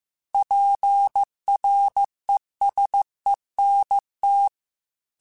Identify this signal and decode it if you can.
Morse code, heard as a keyed tone, about 15 wpm, 800 Hz. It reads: PRESENT
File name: Technical Subjects / Technical Subjects-mp3s / Gates Intro @13